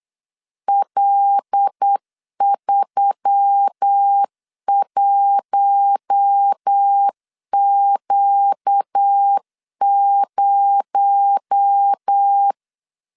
Morse code, heard as a keyed tone, about 8 wpm, 800 Hz. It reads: L31Q0
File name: CW / DET / 1 6 1500C S B 8wpm 5142